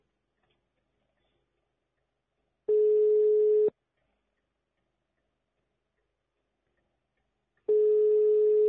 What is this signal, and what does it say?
Ringback tone (Europe, 425 Hz).